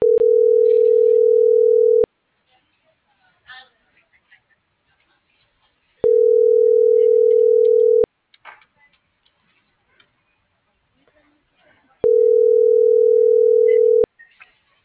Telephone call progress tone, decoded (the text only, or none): ringback tone